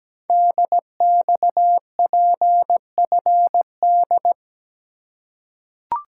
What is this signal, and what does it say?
Morse code, heard as a keyed tone, about 17 wpm, 700 Hz. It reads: DXPFD
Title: Code Groups - 5 Letters - 17wpm